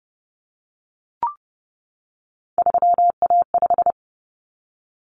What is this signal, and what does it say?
Morse code, heard as a keyed tone, about 30 wpm, 700 Hz. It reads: 3A5
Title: Common Cut Numbers - 3-Digit 30wpm